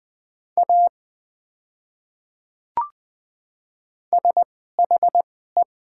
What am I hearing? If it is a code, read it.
Morse code, heard as a keyed tone, about 20 wpm, spaced as for 15 wpm, 700 Hz. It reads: A SHE
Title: 2x Character Spacing - Top 100 Words - 20wpm